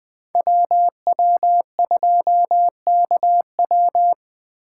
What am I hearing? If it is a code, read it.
Morse code, heard as a keyed tone, about 20 wpm, 700 Hz. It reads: WW2KW
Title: Speed Racing - USA 2x2 Call Signs 17wpm